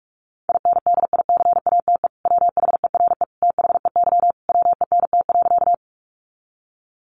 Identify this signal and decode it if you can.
Morse code, heard as a keyed tone, about 44 wpm, 700 Hz. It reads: INDICATE WHERE THEY WENT.
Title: Sentences from Top 1300 Words 45wpm